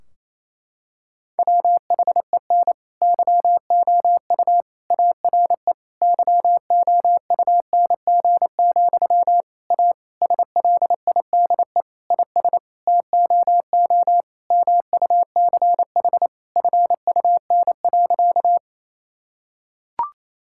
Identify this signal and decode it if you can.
Morse code, heard as a keyed tone, about 27 wpm, 700 Hz. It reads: WHEN YOU ARE YOUNG, A SLIDE IS TOO MUCH FUN.
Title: Sentences from Top 1600 Words 28wpm